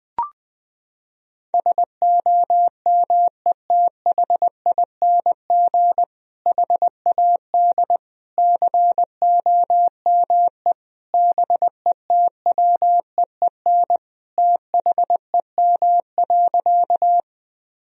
Morse code, heard as a keyed tone, about 20 wpm, 700 Hz. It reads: SOMETHING HAD COME BETWEEN THEM.
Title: Sentences from Top 1000 Words - Review Part 1 - 20wpm